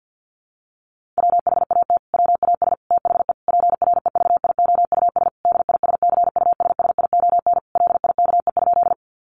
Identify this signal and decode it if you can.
Morse code, heard as a keyed tone, about 50 wpm, 700 Hz. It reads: WHAT WAS THE PREVIOUS DISCUSSION LIKE?
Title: Sentences from Top 1500 Words 50wpm